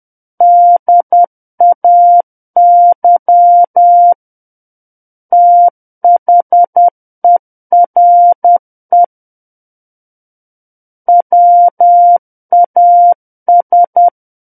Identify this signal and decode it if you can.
Morse code, heard as a keyed tone, about 10 wpm, 700 Hz. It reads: DAY THERE WAS
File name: Book One / 01-Chapter 1 - 10 WPM